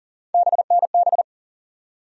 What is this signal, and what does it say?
Morse code, heard as a keyed tone, about 40 wpm, 700 Hz. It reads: BNB
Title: USA Call Sign Postfixes - 3 Letters 40wpm